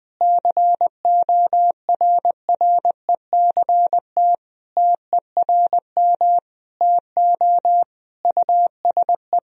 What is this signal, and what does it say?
Morse code, heard as a keyed tone, about 21 wpm, 700 Hz. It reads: CORRECT TERM TO USE
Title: Sentences from Top 1000 Words - Review Part 4 - 20wpm